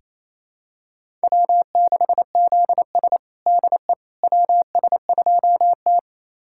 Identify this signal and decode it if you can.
Morse code, heard as a keyed tone, about 28 wpm, 700 Hz. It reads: W6ZS DE WS2T
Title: USA Callsign DE Callsign - 4 Characters - 28wpm